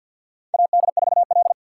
Morse code, heard as a keyed tone, about 53 wpm, 700 Hz. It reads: AD4L
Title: USA Call Signs - 4x Character Spacing - 50wpm